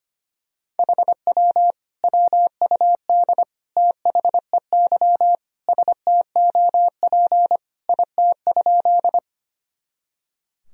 Morse code, heard as a keyed tone, about 25 wpm, 700 Hz. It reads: HW WUD THEY STOP IT?
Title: Sentences from Top 300 Words and Abbreviations 25wpm